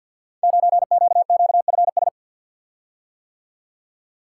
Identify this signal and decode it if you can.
Morse code, heard as a keyed tone, about 48 wpm, 700 Hz. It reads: 9QQFS
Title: Code Groups - 5 Letter:Numbers - 50wpm